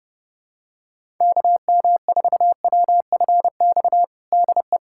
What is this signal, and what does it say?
Morse code, heard as a keyed tone, about 30 wpm, 700 Hz. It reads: KM4WFX DE
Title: USA Callsign DE Callsign - 6 Characters - 30wpm